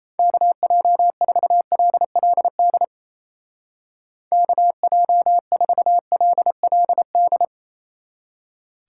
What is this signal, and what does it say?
Morse code, heard as a keyed tone, about 37 wpm, 700 Hz. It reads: KJ4LLD KJ4LLD